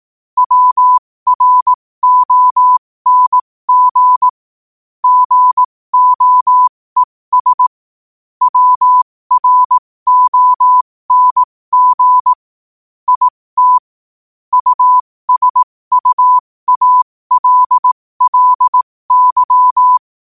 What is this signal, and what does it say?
Morse code, heard as a keyed tone, about 18 wpm, spaced as for 15 wpm, 1000 Hz. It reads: WRONG GOES WRONG IT USUALLY
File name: qotdfiles / 15wpm